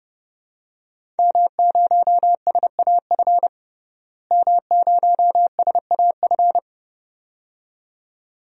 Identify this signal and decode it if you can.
Morse code, heard as a keyed tone, about 32 wpm, 700 Hz. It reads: M0SAF M0SAF
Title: DXCC-100 Call Signs - 5 Characters - Repeated 2x - 30wpm